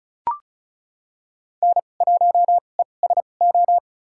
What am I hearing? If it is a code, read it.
Morse code, heard as a keyed tone, about 36 wpm, spaced as for 25 wpm, 700 Hz. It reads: N1ESO
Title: USA Call Signs - 5 Characters - 2x Character Spacing - 35wpm